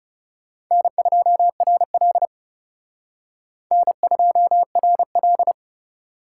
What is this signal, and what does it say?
Morse code, heard as a keyed tone, about 36 wpm, 700 Hz. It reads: N2RL N2RL
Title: Speed Racing - USA 1x2 Call Signs 30wpm